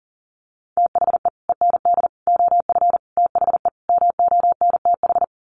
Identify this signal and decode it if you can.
Morse code, heard as a keyed tone, about 40 wpm, 700 Hz. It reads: THE END OF THE MONTH
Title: Sets of 5 Words Encore 40wpm